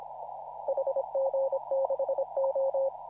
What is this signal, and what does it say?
Morse code, heard as a keyed tone, about 26 wpm, 540 Hz. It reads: HG6O
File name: CW / RHOB CW 2017 HA6P 0001